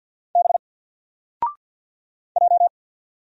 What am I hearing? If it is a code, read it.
Morse code, heard as a keyed tone, about 50 wpm, 700 Hz. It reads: B J
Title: Single Letter-Number 50wpm